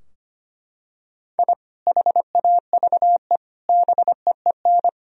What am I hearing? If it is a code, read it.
Morse code, heard as a keyed tone, about 25 wpm, 700 Hz. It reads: I HAVE BEEN
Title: Sentences from Top 1000 Words - Review Part 3 - 25wpm